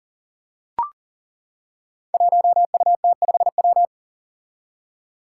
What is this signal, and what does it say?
Morse code, heard as a keyed tone, about 40 wpm, 700 Hz. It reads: 1UT5W
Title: Five Challenging Letter-Number Combinations 40wpm